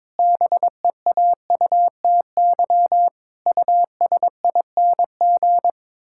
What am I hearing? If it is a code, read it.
Morse code, heard as a keyed tone, about 22 wpm, 700 Hz. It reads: BEAUTY USING
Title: Pangram Challenge v2 - 22wpm